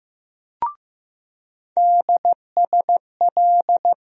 Morse code, heard as a keyed tone, about 15 wpm, 700 Hz. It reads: DSL
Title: USA Call Sign Postfixes - 3 Letters 15wpm